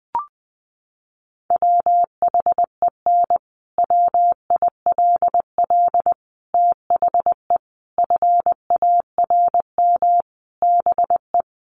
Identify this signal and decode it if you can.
Morse code, heard as a keyed tone, about 20 wpm, 700 Hz. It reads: WHEN WILL THE FARM BE
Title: Sentences from Top 1400 Words 20wpm